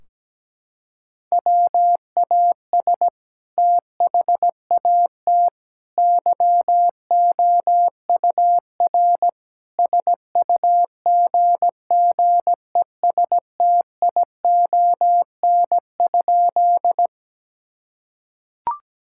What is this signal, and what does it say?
Morse code, heard as a keyed tone, about 17 wpm, 700 Hz. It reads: WAS THAT YOUR SUGGESTION?
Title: Sentences from Top 1600 Words 17wpm